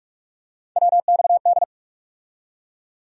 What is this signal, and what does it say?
Morse code, heard as a keyed tone, about 46 wpm, 700 Hz. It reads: WXD
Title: USA Call Sign Postfixes - 3 Letters 45wpm